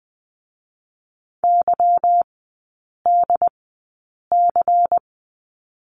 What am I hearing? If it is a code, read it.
Morse code, heard as a keyed tone, about 21 wpm, spaced as for 8 wpm, 700 Hz. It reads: YDC
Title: 145 Lesson 24e - Letter y - Three Characters 20wpm